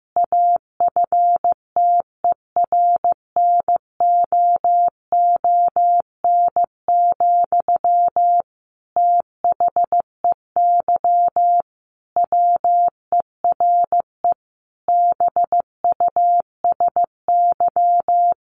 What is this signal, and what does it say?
Morse code, heard as a keyed tone, about 15 wpm, 700 Hz. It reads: AFTERNOON, THEY WERE BUSY
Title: Sentences from Top 1500 Words 15wpm